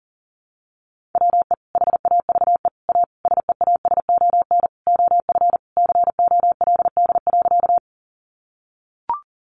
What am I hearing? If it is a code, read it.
Morse code, heard as a keyed tone, about 40 wpm, 700 Hz. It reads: WE HAVE A SEASON OF COLD.
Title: Sentences from Top 1400 Words 40wpm